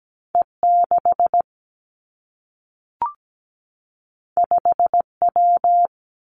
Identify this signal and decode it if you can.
Morse code, heard as a keyed tone, about 17 wpm, 700 Hz. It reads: E 6 5 W
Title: Two Challenging Letter-Number Combinations 17wpm